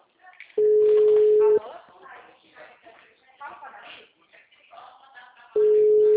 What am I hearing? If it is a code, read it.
Ringback tone (Europe, 425 Hz), with speech.
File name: out- -2010- - - .48093